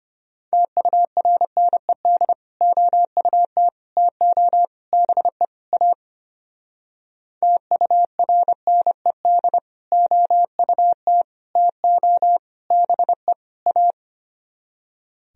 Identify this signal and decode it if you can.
Morse code, heard as a keyed tone, about 31 wpm, 700 Hz. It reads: TURNED OUT TO BE A TURNED OUT TO BE A